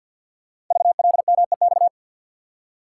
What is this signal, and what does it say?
Morse code, heard as a keyed tone, about 50 wpm, 700 Hz. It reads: VLMEX
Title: Code Groups - 5 Letters - 50wpm